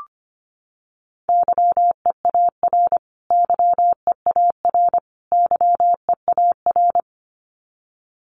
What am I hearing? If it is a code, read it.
Morse code, heard as a keyed tone, about 25 wpm, 700 Hz. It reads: YEAR YEAR YEAR
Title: Top 300 Words Repeated 25wpm